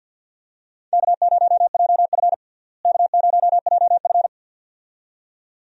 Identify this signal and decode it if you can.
Morse code, heard as a keyed tone, about 49 wpm, 700 Hz. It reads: K0JF K0JF
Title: USA Call Signs - 4 Characters - Repeated 2x - 50wpm